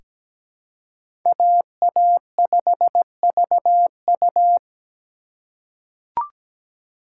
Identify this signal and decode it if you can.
Morse code, heard as a keyed tone, about 17 wpm, 700 Hz. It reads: AA5VU
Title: K1USN SST QSO Practice - 17wpm